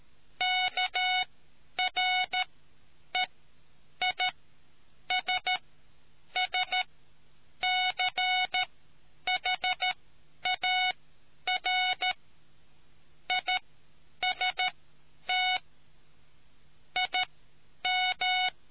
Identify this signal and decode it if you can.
Morse code, heard as a keyed tone, about 13 wpm, spaced as for 8 wpm, 740 Hz. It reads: KREISSCHAR IST IM